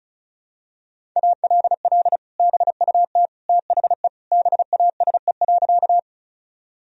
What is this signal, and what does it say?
Morse code, heard as a keyed tone, about 36 wpm, 700 Hz. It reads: ALL BUT THE BASE.
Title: Sentences from Top 1000 Words - Review Part 1 - 35wpm